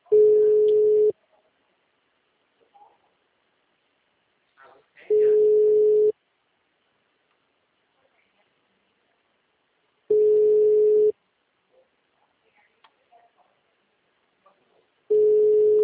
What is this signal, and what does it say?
Ringback tone (Europe, 425 Hz).
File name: out- -2027- - - .3273